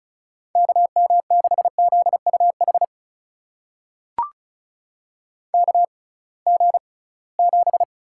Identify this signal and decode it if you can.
Morse code, heard as a keyed tone, about 36 wpm, spaced as for 23 wpm, 700 Hz. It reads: KM6ZUH KG7
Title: USA Call Signs - 6 Characters - 6x Character Spacing - 35wpm